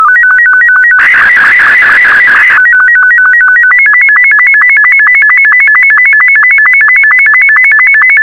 SSTV (slow-scan television), Martin 2.
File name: sstv / M2-VK4MSL